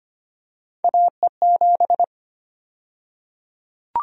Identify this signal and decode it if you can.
Morse code, heard as a keyed tone, about 25 wpm, 700 Hz. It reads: AE7
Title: USA Call Sign Prefixes - 25wpm